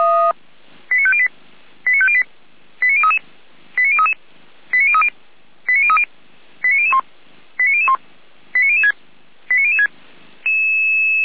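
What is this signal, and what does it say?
Five-tone selective call (ZVEI), heard as quick runs of five tones, tones of about 70 ms. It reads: 89508 89508 89033 89033 89030 89030 89001 89001 89006 89006
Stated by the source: ZVEI five-tone call 89017 received by SDR.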